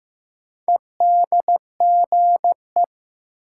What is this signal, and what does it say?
Morse code, heard as a keyed tone, about 15 wpm, 700 Hz. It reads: EDGE
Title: Sets of 3 Words - New Word Drill - 15wpm